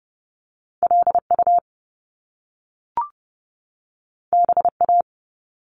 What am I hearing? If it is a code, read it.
Morse code, heard as a keyed tone, about 30 wpm, 700 Hz. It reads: LU BA
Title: Top Two Letter Combinations in Words 30wpm